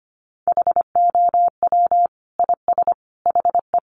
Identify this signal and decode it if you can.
Morse code, heard as a keyed tone, about 25 wpm, 700 Hz. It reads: HOW IS HE
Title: Sentences from Top 100 Words 25wpm